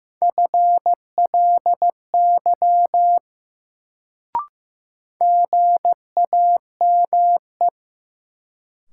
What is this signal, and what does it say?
Morse code, heard as a keyed tone, about 15 wpm, 700 Hz. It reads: FLY GAME
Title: Sentences from Top 500 Words - New Word Drill - 15wpm